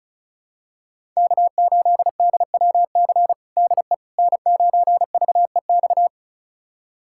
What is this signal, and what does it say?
Morse code, heard as a keyed tone, about 36 wpm, 700 Hz. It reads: K8DWC DE N9VEX
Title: USA Callsign DE Callsign - 5 Characters - 35wpm